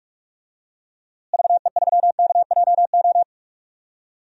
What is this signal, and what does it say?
Morse code, heard as a keyed tone, about 46 wpm, 700 Hz. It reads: VE3KJO
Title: DXCC-100 Call Signs - 6 Characters - Repeated 2x - 45wpm